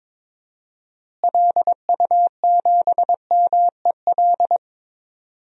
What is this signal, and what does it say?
Morse code, heard as a keyed tone, about 21 wpm, 700 Hz. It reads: LU7MEL
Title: DXCC-100 Call Signs - 6 Characters - Repeated 3x - 22wpm